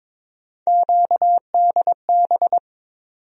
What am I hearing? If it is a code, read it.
Morse code, heard as a keyed tone, about 22 wpm, 700 Hz. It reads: QDB
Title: USA Call Sign Postfixes - 3 Letters 22wpm